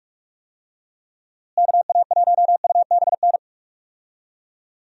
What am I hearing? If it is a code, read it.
Morse code, heard as a keyed tone, about 41 wpm, 700 Hz. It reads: KA1UBN
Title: USA Call Signs - 6 Characters - Repeated 2x - 45wpm